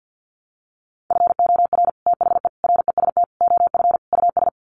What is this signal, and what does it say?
Morse code, heard as a keyed tone, about 50 wpm, 700 Hz. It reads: FOR THE REST OF US
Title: Sets of 5 Words Encore 50wpm